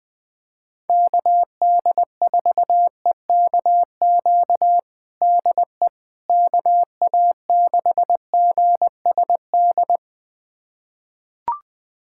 Morse code, heard as a keyed tone, about 21 wpm, 700 Hz. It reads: KD4EKQ DE KA6GSD
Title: USA Callsign DE Callsign - 6 Characters - 20wpm